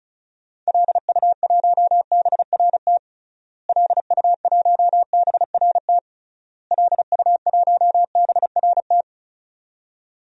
Morse code, heard as a keyed tone, about 36 wpm, 700 Hz. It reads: LU1BRT LU1BRT LU1BRT
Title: DXCC-100 Call Signs - 6 Characters - Repeated 3x - 35wpm